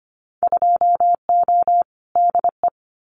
Morse code, heard as a keyed tone, about 25 wpm, 700 Hz. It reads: 2O DE
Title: Winter Field Day Exchanges Encore 25wpm